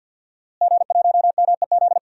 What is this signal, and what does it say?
Morse code, heard as a keyed tone, about 49 wpm, 700 Hz. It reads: G1MEZ
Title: DXCC-100 Call Signs - 5 Characters - Repeated 2x - 50wpm